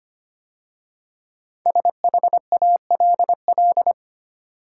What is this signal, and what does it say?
Morse code, heard as a keyed tone, about 25 wpm, 700 Hz. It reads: SHALL
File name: 5 Letter Words - First 3 Letters Given - 25wpm